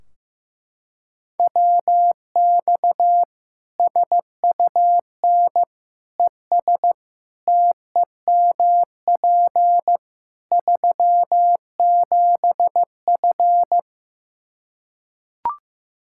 Morse code, heard as a keyed tone, about 15 wpm, 700 Hz. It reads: WX SUN ES TEMP 37F
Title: Practice QSO - Exchange 2 - Info - 15wpm